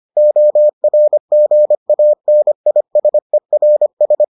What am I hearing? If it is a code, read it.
Morse code, heard as a keyed tone, about 25 wpm, 600 Hz. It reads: ORGANISERS